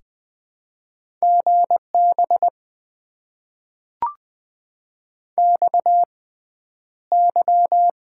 Morse code, heard as a keyed tone, about 20 wpm, spaced as for 10 wpm, 700 Hz. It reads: GB XY
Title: Top 90 QSO Elements - 6x Character Spacing - 20wpm